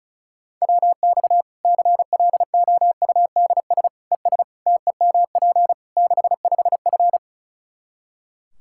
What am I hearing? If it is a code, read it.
Morse code, heard as a keyed tone, about 36 wpm, 700 Hz. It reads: WX CLOUDS ES TEMP 65F
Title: Practice QSO - Exchange 2 - Info - 35wpm